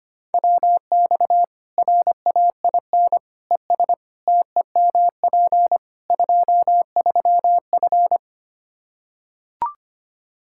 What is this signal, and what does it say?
Morse code, heard as a keyed tone, about 25 wpm, 700 Hz. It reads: WX RAIN ES TEMP 23F
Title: K1USN SST QSO Practice 25wpm